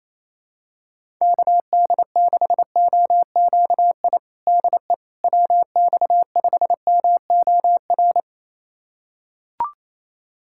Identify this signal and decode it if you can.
Morse code, heard as a keyed tone, about 27 wpm, 700 Hz. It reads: KD6OQI DE WX5MOR
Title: USA Callsign DE Callsign - 6 Characters - 28wpm